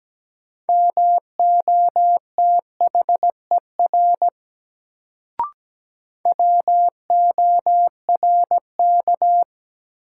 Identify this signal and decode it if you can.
Morse code, heard as a keyed tone, about 17 wpm, 700 Hz. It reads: MOTHER WORK
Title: Sentences from Top 200 Words - New Word Drill - 17wpm